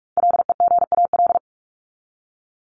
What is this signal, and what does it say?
Morse code, heard as a keyed tone, about 44 wpm, 700 Hz. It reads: LEGAL
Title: Top 1001-1250 Words 45wpm